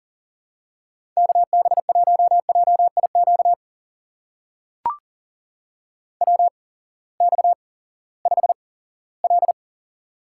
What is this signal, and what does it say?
Morse code, heard as a keyed tone, about 40 wpm, spaced as for 20 wpm, 700 Hz. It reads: KB1JIQ WX5L
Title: USA Call Signs - 8x Character Spacing - 40wpm